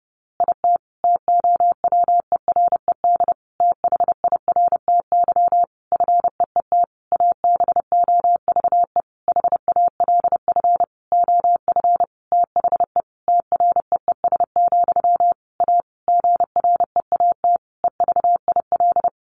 Morse code, heard as a keyed tone, about 30 wpm, 700 Hz. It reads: IT TOWERED THIRTY FEET ABOVE HALF OF THE TREES, A GREAT EVIL
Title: A Sound Of Thunder 30wpm